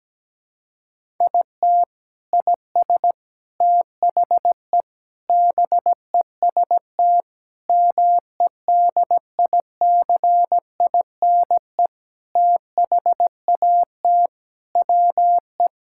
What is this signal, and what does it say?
Morse code, heard as a keyed tone, about 17 wpm, 700 Hz. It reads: IT IS THE BEST MEDICINE THAT WE
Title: Sentences from Top 1000 Words - Review Part 6 - 17wpm